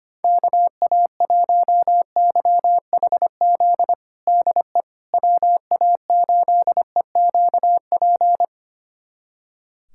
Morse code, heard as a keyed tone, about 25 wpm, 700 Hz. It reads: KA1YHZ DE WA8EQP
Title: USA Callsign DE Callsign - 6 Characters - 25wpm